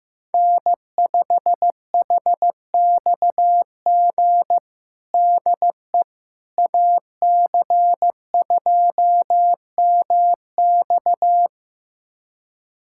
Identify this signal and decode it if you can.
Morse code, heard as a keyed tone, about 16 wpm, 700 Hz. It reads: N5HXG DE AC2MX
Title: USA Callsign DE Callsign - 5 Characters - 15wpm